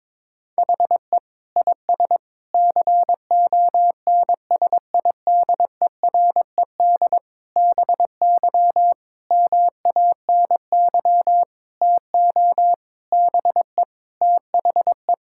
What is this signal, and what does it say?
Morse code, heard as a keyed tone, about 22 wpm, 700 Hz. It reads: HE IS CONSIDERED BY MANY TO BE THE
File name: Sentences with 7-Word Phrases - 22wpm